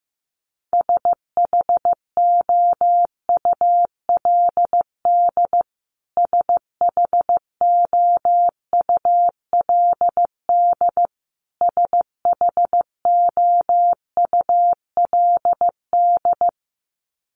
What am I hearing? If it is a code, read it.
Morse code, heard as a keyed tone, about 15 wpm, 700 Hz. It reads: SHOULD SHOULD SHOULD
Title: Top 500 Words Repeated 15wpm